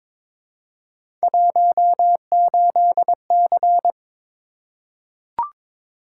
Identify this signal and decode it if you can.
Morse code, heard as a keyed tone, about 22 wpm, 700 Hz. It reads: 18C
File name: practice / 22wpm / Three Letter-Number Combinations - 22wpm